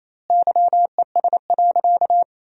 Morse code, heard as a keyed tone, about 28 wpm, 700 Hz. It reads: YES.
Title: 2x Word-Spacing - Sentences from Top 500 Words - 28wpm